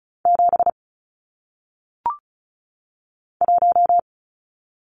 Morse code, heard as a keyed tone, about 36 wpm, 700 Hz. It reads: 7 1
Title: Numbers - 1-Digit 35wpm